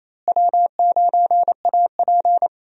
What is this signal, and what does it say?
Morse code, heard as a keyed tone, about 27 wpm, 700 Hz. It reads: W9AP
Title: USA Call Signs - 4 Characters - Repeated 3x - 28wpm